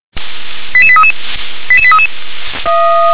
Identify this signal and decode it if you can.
Five-tone selective call (ZVEI), heard as quick runs of five tones, tones of about 70 ms. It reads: 88833 88833